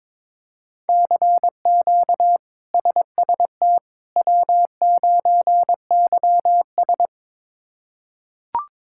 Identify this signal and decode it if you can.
Morse code, heard as a keyed tone, about 22 wpm, 700 Hz. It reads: CQ SST W9YS
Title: K1USN SST QSO Practice - 22wpm